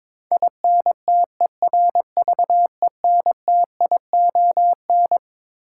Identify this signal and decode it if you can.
Morse code, heard as a keyed tone, about 22 wpm, 700 Hz. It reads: INTERVENTION
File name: Twelve Letter Words - 22wpm